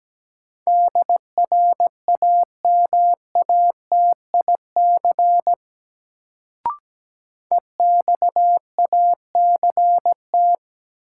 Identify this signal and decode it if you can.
Morse code, heard as a keyed tone, about 17 wpm, 700 Hz. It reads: DRAMATIC EXACT
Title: Sentences from Top 1300 Words - New Word Drill Alt - 17wpm